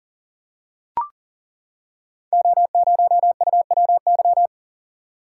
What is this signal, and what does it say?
Morse code, heard as a keyed tone, about 40 wpm, 690 Hz. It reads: O0UWY
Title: Code Groups - 5 Letter:Numbers - 40wpm